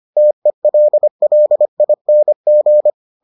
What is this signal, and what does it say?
Morse code, heard as a keyed tone, about 25 wpm, 600 Hz. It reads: TELLING